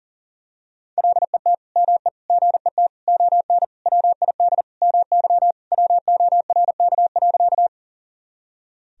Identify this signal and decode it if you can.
Morse code, heard as a keyed tone, about 40 wpm, 700 Hz. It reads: LET ME GET ON WID MY WORK.
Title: Sentences from Top 300 Words and Abbreviations 40wpm